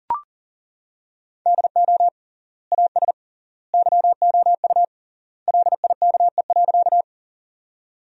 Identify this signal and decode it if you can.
Morse code, heard as a keyed tone, about 40 wpm, 700 Hz. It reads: DO AS YOU LIKE.
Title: 3x Word-Spacing - Sentences from Top 100 Words - 40wpm